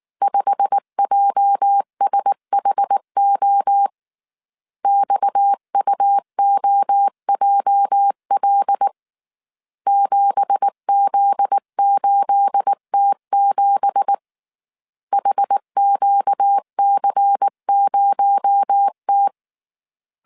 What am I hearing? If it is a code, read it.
Morse code, heard as a keyed tone, about 19 wpm, 800 Hz. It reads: 5JSHO XUOJL 7Z8T7 HQC0T
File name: CW / DET / 1 9 1500C S A 18wpm 2459